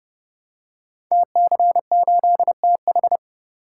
Morse code, heard as a keyed tone, about 30 wpm, 700 Hz. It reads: TC8TH
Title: Code Groups - 5 Letter:Numbers - 30wpm